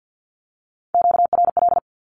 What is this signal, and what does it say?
Morse code, heard as a keyed tone, about 50 wpm, 700 Hz. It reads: QRL
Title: Q-Codes Repeated 3x 50wpm